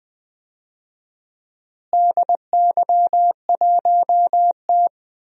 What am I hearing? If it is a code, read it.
Morse code, heard as a keyed tone, about 20 wpm, 700 Hz. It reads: DY1T
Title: DXCC-100 Call Signs - 4 Characters - Repeated 3x - 20wpm